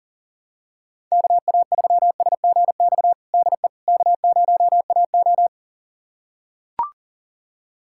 Morse code, heard as a keyed tone, about 40 wpm, 700 Hz. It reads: KA3SGX DE K0AO
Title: USA Callsign DE Callsign 40wpm